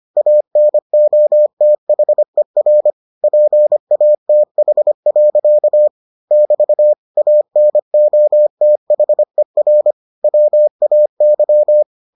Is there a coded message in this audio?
Yes: Morse code from 0.2 to 11.8 s, heard as a keyed tone, about 25 wpm, 600 Hz. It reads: ANOTHER PATH. <BT> ANOTHER WAY